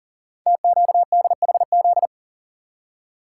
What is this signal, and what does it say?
Morse code, heard as a keyed tone, about 41 wpm, 700 Hz. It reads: TQDHZ
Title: Code Groups - 5 Letter:Numbers - 40wpm